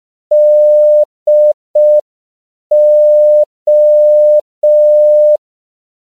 Morse code, heard as a keyed tone, about 5 wpm, 600 Hz. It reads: DO
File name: CW / 5PPM